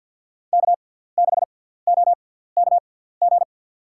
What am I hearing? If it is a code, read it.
Morse code, heard as a keyed tone, about 50 wpm, spaced as for 17 wpm, 700 Hz. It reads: K6OKG
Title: USA Call Signs - 5 Characters - 6x Character Spacing - 50wpm